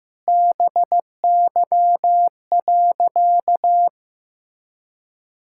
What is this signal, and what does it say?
Morse code, heard as a keyed tone, about 15 wpm, 700 Hz. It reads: BY.
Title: Sentences from Top 1000 Words - Review Part 5 - 15wpm